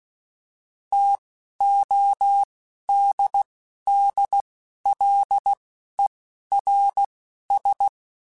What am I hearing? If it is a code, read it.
Morse code, heard as a keyed tone, about 16 wpm, spaced as for 12 wpm, 800 Hz. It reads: TODDLERS